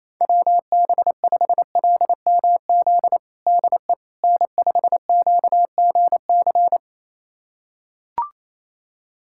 Morse code, heard as a keyed tone, about 28 wpm, 700 Hz. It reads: WB5LMZ DE N5QGC
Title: USA Callsign DE Callsign 28wpm